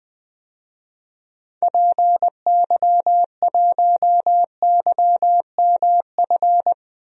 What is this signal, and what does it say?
Morse code, heard as a keyed tone, about 19 wpm, 700 Hz. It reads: PY1YMF
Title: DXCC-100 Call Signs - 6 Characters - Repeated 2x - 20wpm